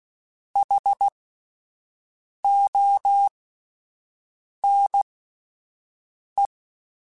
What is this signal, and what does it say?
Morse code, heard as a keyed tone, about 16 wpm, spaced as for 5 wpm, 800 Hz. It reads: HONE
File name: Alphabetical / Alphabetical mp3s / Hs @5